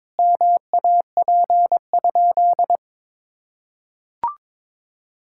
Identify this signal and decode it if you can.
Morse code, heard as a keyed tone, about 22 wpm, 700 Hz. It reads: MAP?
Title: Sentences from Top 1000 Words - Review Part 5 - 22wpm